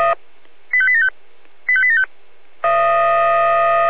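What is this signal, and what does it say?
Five-tone selective call (ZVEI), heard as quick runs of five tones, tones of about 70 ms. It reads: 85875 85875
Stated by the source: Received on an SDR.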